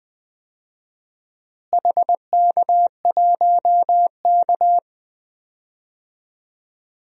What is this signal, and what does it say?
Morse code, heard as a keyed tone, about 21 wpm, 700 Hz. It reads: HK1K
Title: DXCC-100 Call Signs - 4 Characters - Repeated 3x - 20wpm